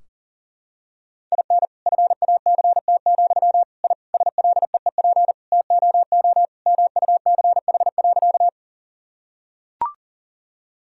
Morse code, heard as a keyed tone, about 40 wpm, 700 Hz. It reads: IN FACT, I SLEEP TOO MUCH.
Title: Sentences from Top 1000 Words - Review Part 3 - 40wpm